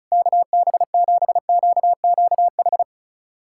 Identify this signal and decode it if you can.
Morse code, heard as a keyed tone, about 36 wpm, 700 Hz. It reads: KB7QQH
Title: USA Callsign DE Callsign 35wpm